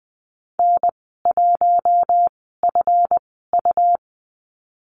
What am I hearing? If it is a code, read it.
Morse code, heard as a keyed tone, about 20 wpm, spaced as for 15 wpm, 700 Hz. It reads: N1FU
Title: 179 Lesson 29g - Slash - Call Signs 20wpm